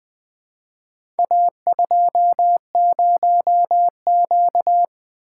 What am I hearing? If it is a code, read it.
Morse code, heard as a keyed tone, about 20 wpm, 700 Hz. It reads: A20Q
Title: Four Letter-Number Combinations - 20wpm